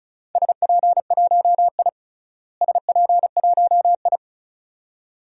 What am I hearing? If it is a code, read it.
Morse code, heard as a keyed tone, about 34 wpm, 700 Hz. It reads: SP1I SP1I
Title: DXCC-100 Call Signs - 4 Characters - Repeated 2x - 35wpm